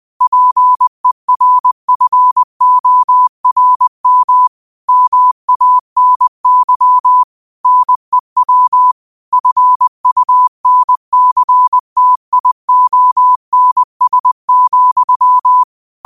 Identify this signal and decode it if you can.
Morse code, heard as a keyed tone, about 20 wpm, 1000 Hz. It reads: PERFORM MANY NEW FUNCTIONS,
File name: qotdfiles / 20wpm